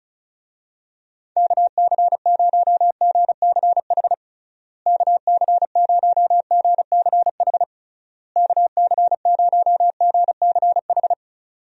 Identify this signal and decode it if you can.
Morse code, heard as a keyed tone, about 36 wpm, 700 Hz. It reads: KC0GCH KC0GCH KC0GCH
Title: USA Call Signs - 6 Characters - Repeated 3x - 35wpm